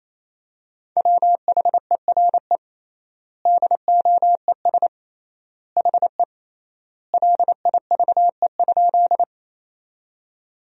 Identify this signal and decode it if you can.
Morse code, heard as a keyed tone, about 27 wpm, 700 Hz. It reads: WHERE DOES HE LIVE?
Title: 3x Word-Spacing - Sentences from Top 200 Words - 28wpm